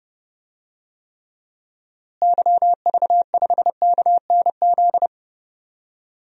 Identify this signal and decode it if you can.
Morse code, heard as a keyed tone, about 32 wpm, 700 Hz. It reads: YV5KNZ
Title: DXCC-100 Call Signs - 6 Characters - Repeated 3x - 30wpm